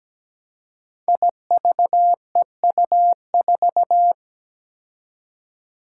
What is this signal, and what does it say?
Morse code, heard as a keyed tone, about 17 wpm, 700 Hz. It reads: IVEU4
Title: Five Challenging Letter-Number Combinations 17wpm